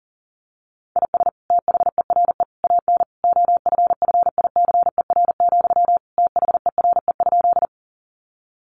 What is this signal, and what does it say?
Morse code, heard as a keyed tone, about 40 wpm, 700 Hz. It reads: IS THERE AN OFFICER, THERE?
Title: Sentences from Top 1200 Words 40wpm